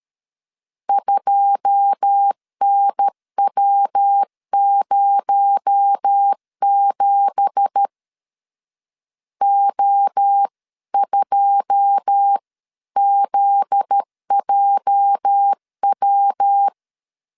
Morse code, heard as a keyed tone, about 13 wpm, 800 Hz. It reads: 2NW07 O2ZJW
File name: CW / DET / 1 9 1500C S C 12wpm 3811